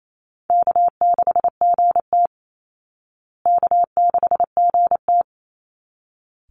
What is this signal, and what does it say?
Morse code, heard as a keyed tone, about 28 wpm, 700 Hz. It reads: K6GT K6GT
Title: Call Signs Repeated 3x - 28wpm